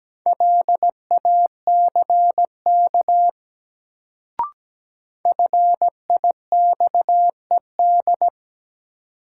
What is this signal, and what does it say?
Morse code, heard as a keyed tone, about 17 wpm, 700 Hz. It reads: LACK FIXED
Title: Sentences from Top 1100 Words - New Word Drill - 17wpm